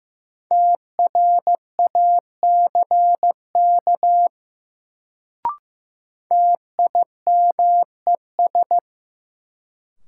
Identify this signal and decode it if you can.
Morse code, heard as a keyed tone, about 15 wpm, 700 Hz. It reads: TRACK TIMES
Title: Sentences from Top 1100 Words - New Word Drill Alt - 15wpm